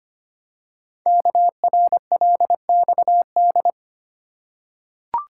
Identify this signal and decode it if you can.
Morse code, heard as a keyed tone, about 25 wpm, 700 Hz. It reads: KRLXD
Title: Code Groups - 5 Letters - 25wpm